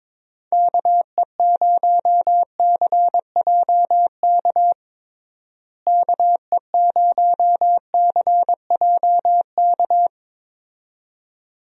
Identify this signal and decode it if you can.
Morse code, heard as a keyed tone, about 22 wpm, 700 Hz. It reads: KE0CJK KE0CJK
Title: USA Call Signs - 6 Characters - Repeated 2x - 22wpm